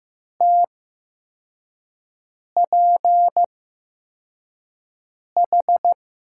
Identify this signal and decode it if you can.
Morse code, heard as a keyed tone, about 15 wpm, spaced as for 4 wpm, 700 Hz. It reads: TPH
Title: USA Call Signs - 6 Characters - 8x Character Spacing - 15wpm